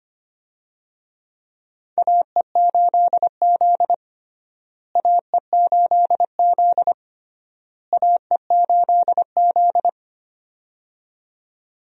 Morse code, heard as a keyed tone, about 25 wpm, 700 Hz. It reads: AE8Z AE8Z AE8Z
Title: USA Call Signs - 4 Characters - Repeated 3x - 25wpm